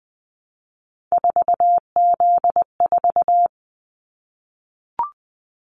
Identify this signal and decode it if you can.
Morse code, heard as a keyed tone, about 20 wpm, 700 Hz. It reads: 4Z4
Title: Callsign Prefixes for Top 50 DXCC - 20wpm